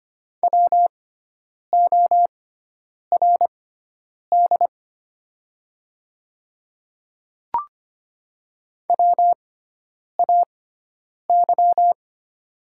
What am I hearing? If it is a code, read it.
Morse code, heard as a keyed tone, about 25 wpm, spaced as for 9 wpm, 700 Hz. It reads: WORD WAY
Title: 6x Character Spacing - Top 200 Words - 25wpm